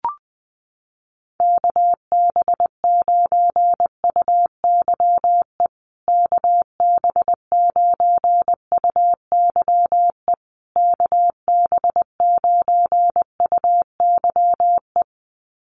Morse code, heard as a keyed tone, about 20 wpm, 700 Hz. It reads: KB9UYE KB9UYE KB9UYE
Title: USA Call Signs Repeated 3x 20wpm